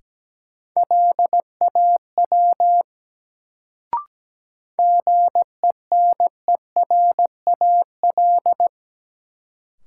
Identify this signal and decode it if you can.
Morse code, heard as a keyed tone, about 17 wpm, 700 Hz. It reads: LAW GENERAL
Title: Sentences from Top 600 Words - New Word Drill - 17wpm